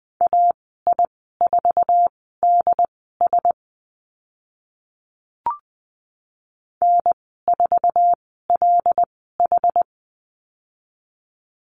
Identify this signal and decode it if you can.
Morse code, heard as a keyed tone, about 20 wpm, spaced as for 15 wpm, 700 Hz. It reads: AI4DS N4LH
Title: 082 Lesson 14g - Number 2 - Call Signs 20wpm